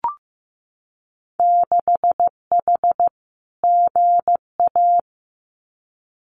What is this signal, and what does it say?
Morse code, heard as a keyed tone, about 15 wpm, 700 Hz. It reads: 6H GA
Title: Winter Field Day Exchanges Encore 15wpm